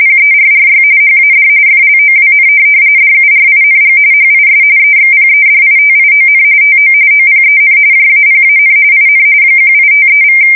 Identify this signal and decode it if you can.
RTTY, about 45.45 baud. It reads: PERRY HAD FLOWN DURING HIS EXPEDITION TO OPEN JAPAN IN 1853=
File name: fsk170 / USS-Missouri